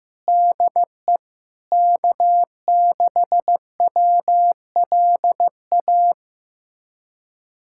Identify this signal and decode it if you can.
Morse code, heard as a keyed tone, about 16 wpm, 700 Hz. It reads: DE K6WLA
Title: USA Callsign DE Callsign - 5 Characters - 15wpm